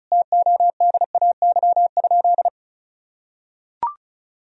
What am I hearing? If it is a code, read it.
Morse code, heard as a keyed tone, about 34 wpm, 700 Hz. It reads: TODAY?
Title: Sentences from Top 1000 Words - Review Part 4 - 35wpm